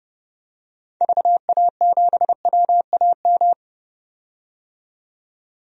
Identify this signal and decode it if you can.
Morse code, heard as a keyed tone, about 31 wpm, 700 Hz. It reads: VA7WAM
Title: DXCC-100 Call Signs - 6 Characters - 30wpm